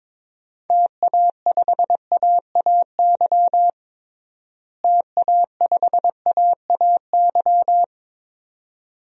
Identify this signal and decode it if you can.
Morse code, heard as a keyed tone, about 22 wpm, 700 Hz. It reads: TA5AAY TA5AAY
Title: DXCC-100 Call Signs - 6 Characters - Repeated 2x - 22wpm